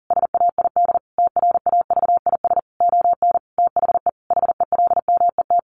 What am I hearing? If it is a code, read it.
Morse code, heard as a keyed tone, about 40 wpm, 700 Hz. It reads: SAID TRAVIS ON THE HELMET
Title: A Sound Of Thunder 40wpm